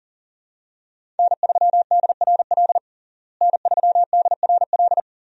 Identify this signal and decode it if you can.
Morse code, heard as a keyed tone, about 37 wpm, 700 Hz. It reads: N3DRL N3DRL
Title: USA Call Signs - 5 Characters - Repeated 2x - 40wpm